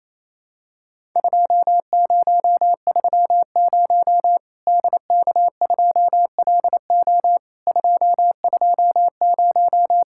Morse code, heard as a keyed tone, about 28 wpm, 700 Hz. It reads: 2030 DK2LO 220
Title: Worked All Europe DX Contest - QTC Exchange - 28wpm